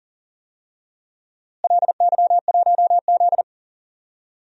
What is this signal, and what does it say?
Morse code, heard as a keyed tone, about 40 wpm, 700 Hz. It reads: LY1Z
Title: DXCC-100 Call Signs - 4 Characters - Repeated 3x - 40wpm